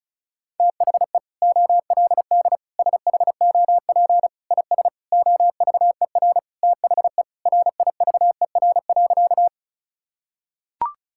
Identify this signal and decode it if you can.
Morse code, heard as a keyed tone, about 36 wpm, 690 Hz. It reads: THE OLD SHOP IS OVER THE RIVER.
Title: Sentences from Top 1600 Words 35wpm